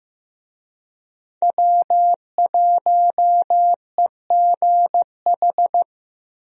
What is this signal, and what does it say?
Morse code, heard as a keyed tone, about 15 wpm, 700 Hz. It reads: W1EGH
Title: USA Call Signs - 5 Characters - 15wpm